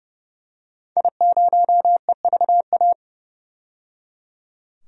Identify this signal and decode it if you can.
Morse code, heard as a keyed tone, about 30 wpm, 700 Hz. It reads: I0EVA
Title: DXCC-100 Call Signs - 5 Characters - 30wpm